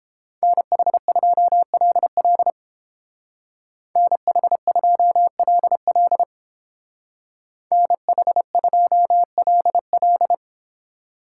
Morse code, heard as a keyed tone, about 30 wpm, 700 Hz. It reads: NH2LL NH2LL NH2LL